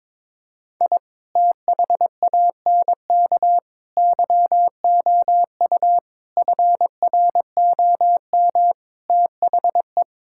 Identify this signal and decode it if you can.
Morse code, heard as a keyed tone, about 22 wpm, 700 Hz. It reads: I THANK YOU FROM THE
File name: practice / 22wpm / Sentences with 8-Word Phrases - 22wpm